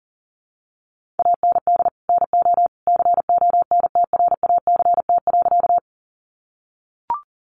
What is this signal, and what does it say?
Morse code, heard as a keyed tone, about 40 wpm, 700 Hz. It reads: AND NO CONTRACT.
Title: Sentences from Top 1400 Words 40wpm